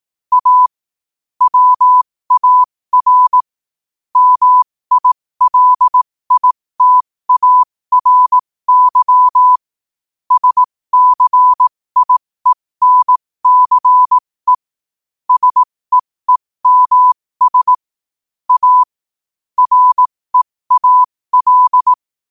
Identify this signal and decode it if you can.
Morse code, heard as a keyed tone, about 18 wpm, spaced as for 15 wpm, 1000 Hz. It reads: A WAR MILITARY SCIENCE SEEMS A REAL
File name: qotdfiles / 15wpm